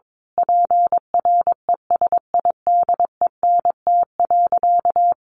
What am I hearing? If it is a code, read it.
Morse code, heard as a keyed tone, about 22 wpm, 700 Hz. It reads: PRESIDENT.
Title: Sentences from Top 1500 Words 22wpm